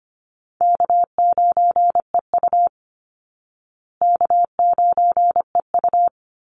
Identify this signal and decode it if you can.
Morse code, heard as a keyed tone, about 25 wpm, 700 Hz. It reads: K9EU K9EU
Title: Call Signs Repeated 3x - 25wpm